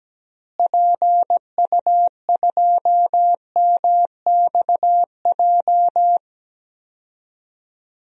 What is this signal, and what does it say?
Morse code, heard as a keyed tone, about 17 wpm, 700 Hz. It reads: PU2MXJ
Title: DXCC-100 Call Signs - 6 Characters - Repeated 2x - 17wpm